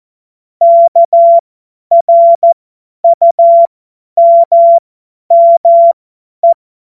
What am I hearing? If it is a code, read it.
Morse code, heard as a keyed tone, about 14 wpm, spaced as for 10 wpm, 680 Hz. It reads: KRUMME
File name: Witz-22-Juli-2025-50-F70